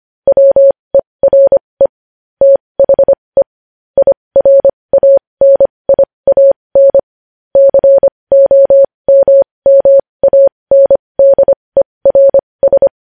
Morse code, heard as a keyed tone, about 25 wpm, spaced as for 20 wpm, 550 Hz. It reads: WERE THE IRANIAN COMMANDERS
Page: News Headlines in Morse Code at 20 WPM